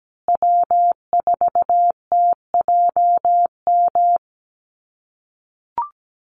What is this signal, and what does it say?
Morse code, heard as a keyed tone, about 17 wpm, 700 Hz. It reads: W4TJM
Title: USA Call Signs Repeated 2x 17wpm